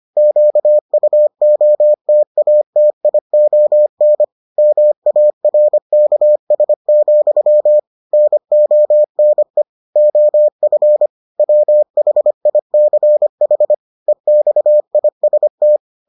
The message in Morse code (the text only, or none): QUOTATION MARKS, NONE OF WHICH EXIST